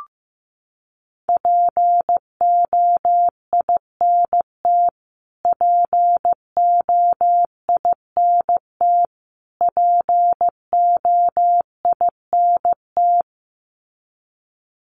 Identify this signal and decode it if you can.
Morse code, heard as a keyed tone, about 15 wpm, 700 Hz. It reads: POINT POINT POINT
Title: Top 500 Words Repeated 15wpm